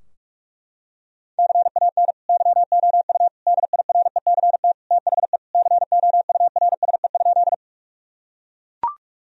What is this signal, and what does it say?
Morse code, heard as a keyed tone, about 43 wpm, 700 Hz. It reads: CAN YOU DIRECT THE COURSE?
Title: Sentences from Top 1000 Words - Review Part 2 - 45wpm